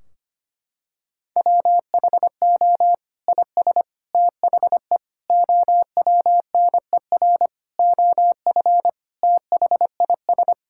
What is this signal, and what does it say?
Morse code, heard as a keyed tone, about 25 wpm, 700 Hz. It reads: WHO IS THE OWNER OF THIS
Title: Sentences from Top 1600 Words 25wpm